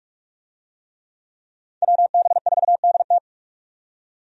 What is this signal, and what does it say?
Morse code, heard as a keyed tone, about 46 wpm, 700 Hz. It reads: WB4DT
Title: USA Call Signs - 5 Characters - 45wpm